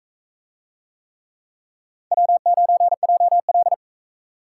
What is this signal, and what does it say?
Morse code, heard as a keyed tone, about 40 wpm, 700 Hz. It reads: W9JL